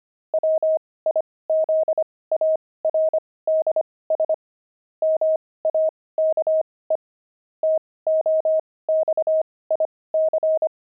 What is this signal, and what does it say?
Morse code, heard as a keyed tone, about 25 wpm, spaced as for 18 wpm, 630 Hz. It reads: WIZARDS MAKE TOXIC
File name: pangram-fs-18-cs-25-grumpy